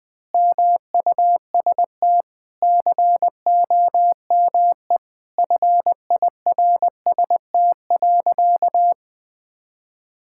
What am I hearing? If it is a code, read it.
Morse code, heard as a keyed tone, about 20 wpm, 700 Hz. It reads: MUST COME FIRST.
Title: Sentences from Top 1000 Words - Review Part 6 - 20wpm